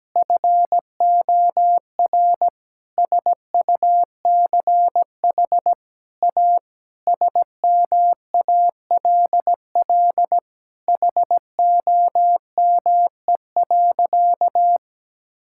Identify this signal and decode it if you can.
Morse code, heard as a keyed tone, about 17 wpm, 700 Hz. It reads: FOR SUCH A SMALL HOME.
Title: Sentences from Top 1000 Words - Review Part 6 - 17wpm